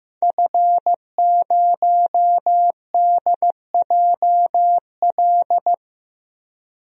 Morse code, heard as a keyed tone, about 15 wpm, 700 Hz. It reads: F0DJL
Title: DXCC-100 Call Signs - 5 Characters - 15wpm